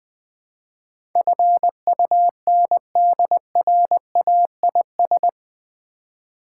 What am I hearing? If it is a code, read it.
Morse code, heard as a keyed tone, about 20 wpm, 700 Hz. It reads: FUNDRAIS
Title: Build A Word - 4 Letters At A Time - 11-Letter Words - 20wpm